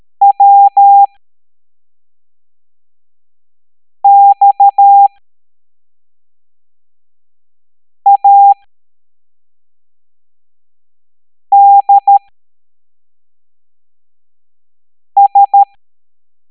Morse code, heard as a keyed tone, about 13 wpm, spaced as for 3 wpm, 800 Hz. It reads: WXADS